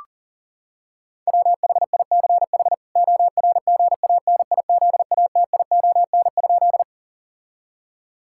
Morse code, heard as a keyed tone, about 40 wpm, 700 Hz. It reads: WHICH ORGANIZATION?
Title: Sentences from Top 1000 Words - Review Part 5 - 40wpm